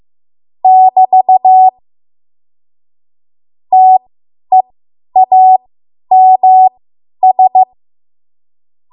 Morse code, heard as a keyed tone, about 15 wpm, spaced as for 10 wpm, 750 Hz. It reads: <BT> TEAMS
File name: Morse / 10WPM